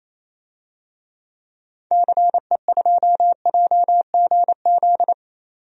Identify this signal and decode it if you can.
Morse code, heard as a keyed tone, about 28 wpm, 700 Hz. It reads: CE2JGZ
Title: DXCC-100 Call Signs - 6 Characters - 28wpm